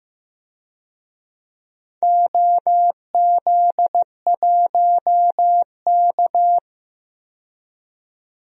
Morse code, heard as a keyed tone, about 15 wpm, 700 Hz. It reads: OZ1K
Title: DXCC-100 Call Signs - 4 Characters - 15wpm